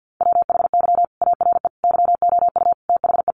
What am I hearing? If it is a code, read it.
Morse code, heard as a keyed tone, about 50 wpm, 700 Hz. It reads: WHY ARE YOU THE
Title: Sentences from Top 1200 Words 50wpm